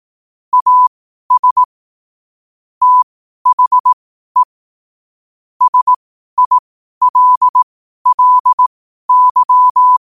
Morse code, heard as a keyed tone, about 18 wpm, spaced as for 12 wpm, 1000 Hz. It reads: AS THE SILLY
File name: qotdfiles / 12wpm